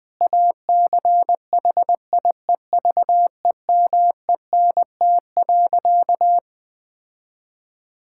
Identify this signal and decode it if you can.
Morse code, heard as a keyed tone, about 21 wpm, 700 Hz. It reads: ACHIEVEMENT.
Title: Difficult Sentences with Hints - 20wpm